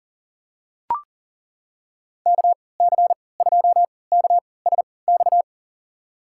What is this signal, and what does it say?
Morse code, heard as a keyed tone, about 40 wpm, spaced as for 23 wpm, 700 Hz. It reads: KC2KSX
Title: USA Call Signs - 3x Character Spacing - 40wpm